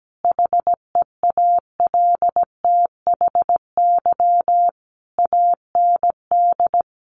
Morse code, heard as a keyed tone, about 17 wpm, 700 Hz. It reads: HEALTHY AND
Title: A Sound Of Thunder 17wpm